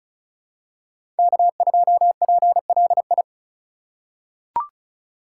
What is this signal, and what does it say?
Morse code, heard as a keyed tone, about 34 wpm, 700 Hz. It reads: K2PLI
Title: USA Call Signs - 3x Character Spacing - 35wpm